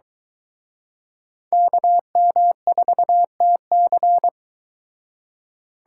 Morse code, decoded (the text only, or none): KM4TC